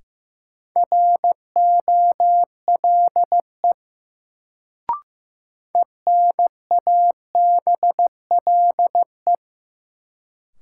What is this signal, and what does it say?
Morse code, heard as a keyed tone, about 15 wpm, 700 Hz. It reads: ROLE ENABLE
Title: Sentences from Top 1500 Words - New Word Drill Alt - 15wpm